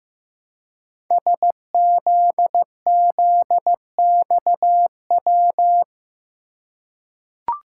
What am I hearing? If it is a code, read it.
Morse code, heard as a keyed tone, about 15 wpm, 700 Hz. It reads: SZZXW
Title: Code Groups - 5 Letters - 15wpm